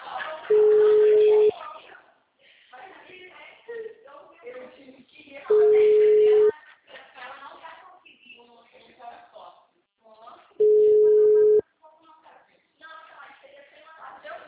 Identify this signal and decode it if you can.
Ringback tone (Europe, 425 Hz), with speech.
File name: out- -2051- - - .28266